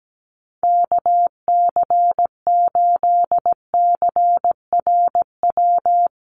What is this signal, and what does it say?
Morse code, heard as a keyed tone, about 17 wpm, 700 Hz. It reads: KC8CRW
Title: USA Call Signs Repeated 2x 17wpm